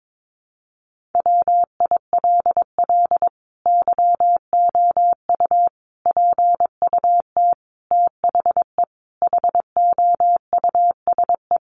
Morse code, heard as a keyed tone, about 22 wpm, 700 Hz. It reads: WILL YOU PUT THE HOUSE
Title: Sentences from Top 1400 Words 22wpm